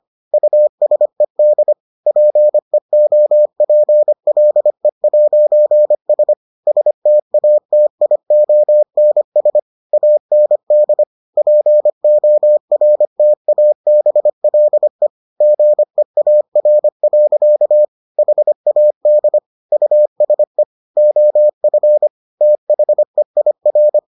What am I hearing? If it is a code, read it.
Morse code, heard as a keyed tone, about 25 wpm, 600 Hz. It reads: USED PEOPLE'S STATIONS AND PORTABLE GEAR. HAD USE OF THEIR